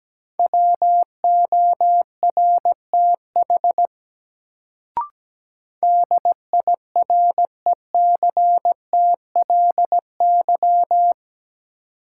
Morse code, heard as a keyed tone, about 17 wpm, 700 Hz. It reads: WORTH DIRECTLY
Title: Sentences from Top 800 Words - New Word Drill - 17wpm